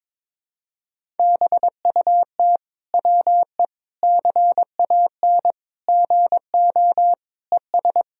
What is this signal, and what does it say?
Morse code, heard as a keyed tone, about 21 wpm, 700 Hz. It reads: BUT WE CAN GO ES
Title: Sentences from Top 100 Words and Abbreviations 22wpm